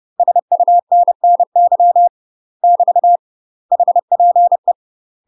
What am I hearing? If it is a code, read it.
Morse code, heard as a keyed tone, about 30 wpm, 700 Hz. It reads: SUNNY <BT> HPE
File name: UK MASTER QSOs TRAINING / 30WPM / UK MASTER QSOs 0005